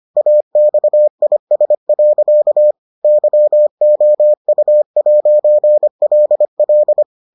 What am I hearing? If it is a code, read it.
Morse code, heard as a keyed tone, about 25 wpm, 600 Hz. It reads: AXIS. YOU'LL